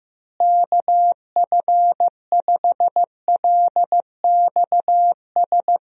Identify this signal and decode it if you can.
Morse code, heard as a keyed tone, about 15 wpm, 700 Hz. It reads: KF5LXS
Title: USA Call Signs - 6 Characters - Repeated 2x - 15wpm